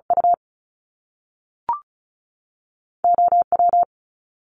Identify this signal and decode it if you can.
Morse code, heard as a keyed tone, about 36 wpm, spaced as for 6 wpm, 700 Hz. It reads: UOW
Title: Top Two Letter Combinations in Words 35wpm